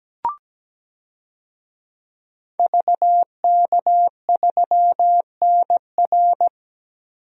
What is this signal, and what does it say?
Morse code, heard as a keyed tone, about 17 wpm, 700 Hz. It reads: VK3NR
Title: DXCC-100 Call Signs - 5 Characters - Repeated 3x - 17wpm